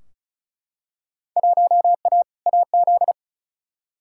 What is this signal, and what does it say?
Morse code, heard as a keyed tone, about 36 wpm, 700 Hz. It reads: 1A AZ
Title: Field Day Exchanges - 35wpm